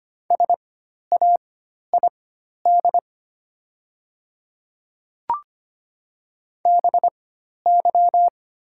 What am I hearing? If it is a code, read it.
Morse code, heard as a keyed tone, about 25 wpm, spaced as for 12 wpm, 690 Hz. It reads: SAID BY
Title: 4x Character Spacing - Top 100 Words - 25wpm